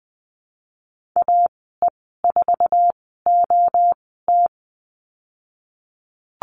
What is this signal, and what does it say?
Morse code, heard as a keyed tone, about 20 wpm, spaced as for 15 wpm, 700 Hz. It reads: AE4OT
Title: 055 Lesson 10g - Letter r - Call Signs 20wpm